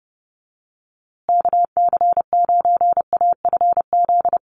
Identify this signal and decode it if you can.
Morse code, heard as a keyed tone, about 30 wpm, 700 Hz. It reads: KC9AFZ
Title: USA Call Signs Repeated 3x 30wpm